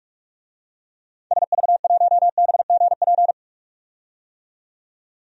Morse code, heard as a keyed tone, about 46 wpm, 700 Hz. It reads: SV1BGP
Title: DXCC-100 Call Signs - 6 Characters - 45wpm